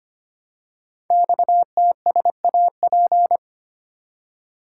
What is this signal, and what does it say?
Morse code, heard as a keyed tone, about 25 wpm, 700 Hz. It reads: XTSAP
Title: Code Groups - 5 Letters - 25wpm